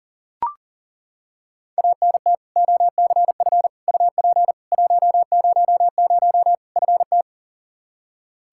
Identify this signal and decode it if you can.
Morse code, heard as a keyed tone, about 40 wpm, 700 Hz. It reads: ANT OCF UP 100 FT
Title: K1USN SST QSO Practice 40wpm